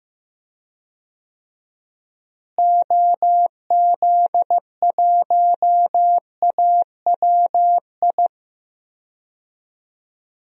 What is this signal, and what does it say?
Morse code, heard as a keyed tone, about 15 wpm, 700 Hz. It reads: OZ1AWI
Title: DXCC-100 Call Signs - 6 Characters - Repeated 3x - 15wpm